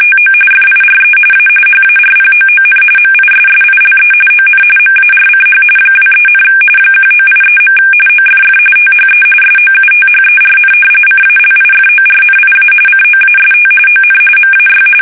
RTTY, about 45.45 baud. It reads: TO FIRE A HARPOON MISSILE ON MARCH 23, 1983. OFF THE SOUTHERN CALIFORNIA COAST ON MAY
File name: fsk850 / battleshipnj